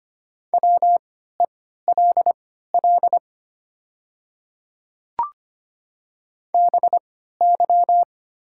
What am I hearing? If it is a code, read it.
Morse code, heard as a keyed tone, about 25 wpm, spaced as for 14 wpm, 700 Hz. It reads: WELL BY
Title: 3x Character Spacing - Top 200 Words - 25wpm